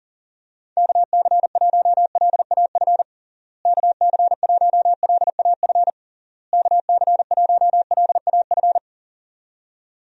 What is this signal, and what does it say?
Morse code, heard as a keyed tone, about 40 wpm, 700 Hz. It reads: KC1LAF KC1LAF KC1LAF
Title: USA Call Signs - 6 Characters - Repeated 3x - 40wpm